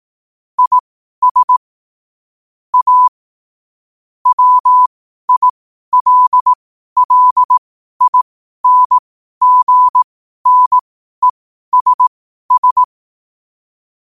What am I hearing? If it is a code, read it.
Morse code, heard as a keyed tone, about 18 wpm, spaced as for 12 wpm, 1000 Hz. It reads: IS A WILLINGNESS
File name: qotdfiles / 12wpm